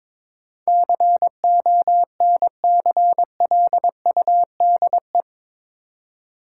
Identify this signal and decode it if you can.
Morse code, heard as a keyed tone, about 22 wpm, 700 Hz. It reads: CONCLUDE
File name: practice / 22wpm / Eight Letter Words - 22wpm